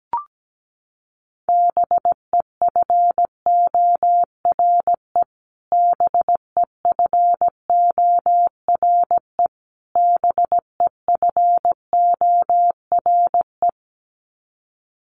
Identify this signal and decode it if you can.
Morse code, heard as a keyed tone, about 17 wpm, 700 Hz. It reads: BEFORE BEFORE BEFORE
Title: Top 300 Words Repeated 17wpm